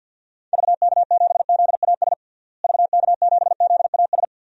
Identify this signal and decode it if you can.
Morse code, heard as a keyed tone, about 52 wpm, 700 Hz. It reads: VK7ZAS VK7ZAS
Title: DXCC-100 Call Signs - 6 Characters - Repeated 2x - 50wpm